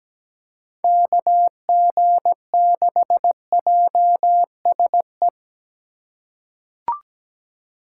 Morse code, heard as a keyed tone, about 17 wpm, 700 Hz. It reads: KG6JSE
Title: USA Call Signs - 6 Characters - 3x Character Spacing - 17wpm